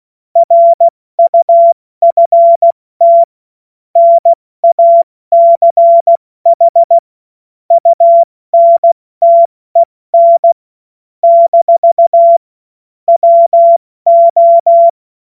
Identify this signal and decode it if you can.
Morse code, heard as a keyed tone, about 16 wpm, 680 Hz. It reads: RUFT NACH UNTEN - WO
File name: Morsekurs / Witze / Witz 16. August 70 F80